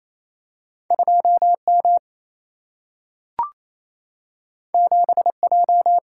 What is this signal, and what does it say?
Morse code, heard as a keyed tone, about 28 wpm, 700 Hz. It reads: 2M 7J
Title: Single-Letter Single-Number 28wpm